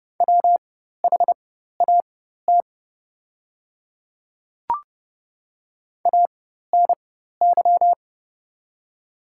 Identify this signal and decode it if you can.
Morse code, heard as a keyed tone, about 31 wpm, spaced as for 14 wpm, 700 Hz. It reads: WHAT ANY
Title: 4x Character Spacing - Top 200 Words - 30wpm